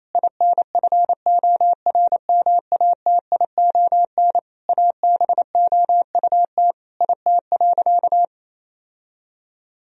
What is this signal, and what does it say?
Morse code, heard as a keyed tone, about 28 wpm, 700 Hz. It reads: INFORMATION ABOUT IT.
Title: Sentences from Top 1000 Words - Review Part 3 - 28wpm